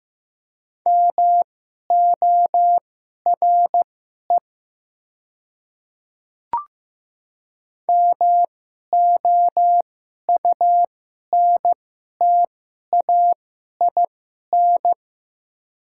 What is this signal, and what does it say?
Morse code, heard as a keyed tone, about 15 wpm, spaced as for 11 wpm, 700 Hz. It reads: MORE MOUNTAIN
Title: 2x Character Spacing - Top 300 Words - 15wpm